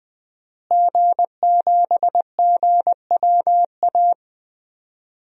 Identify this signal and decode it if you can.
Morse code, heard as a keyed tone, about 20 wpm, 700 Hz. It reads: G7GWA
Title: DXCC-100 Call Signs - 5 Characters - Repeated 3x - 20wpm